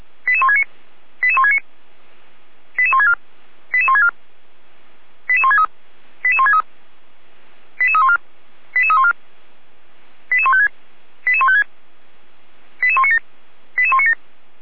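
Five-tone selective call (ZVEI), heard as quick runs of five tones, tones of about 70 ms. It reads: 88169 88169 88164 88164 88163 88163 88314 88314 88156 88156 88197 88197
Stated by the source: ZVEI five-tone selective call received by SDR, from the Feuerwehr Orlamünde fire brigade audio archive.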